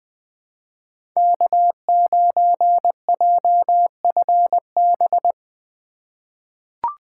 Morse code, heard as a keyed tone, about 20 wpm, 700 Hz. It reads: K9JFB
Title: 2023 POTA Activator Call Signs - 20wpm